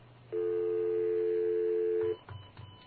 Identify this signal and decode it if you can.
Telephone call progress tone (North America), heard as a telephone network's tone, about 350, 440 Hz. Dial tone.